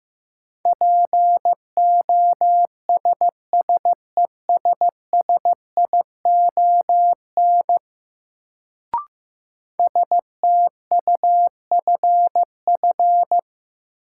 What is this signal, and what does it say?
Morse code, heard as a keyed tone, about 15 wpm, 700 Hz. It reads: POSSESSION STUFF
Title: Sentences from Top 1400 Words - New Word Drill - 15wpm